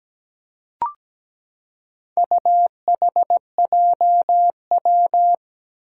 Morse code, heard as a keyed tone, about 17 wpm, 700 Hz. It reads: UHJW
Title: Four Challenging Letter-Number Combinations 17wpm